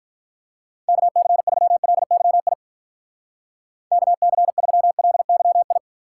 Morse code, heard as a keyed tone, about 50 wpm, 700 Hz. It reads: KC3LYI KC3LYI